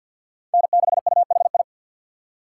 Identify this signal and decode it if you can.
Morse code, heard as a keyed tone, about 50 wpm, 700 Hz. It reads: N6UHI
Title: USA Call Signs - 5 Characters - 4x Character Spacing - 50wpm